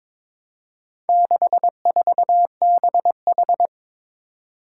Morse code, heard as a keyed tone, about 22 wpm, 700 Hz. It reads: 64BH
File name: practice / 22wpm / Four Letter-Number Combinations - 22wpm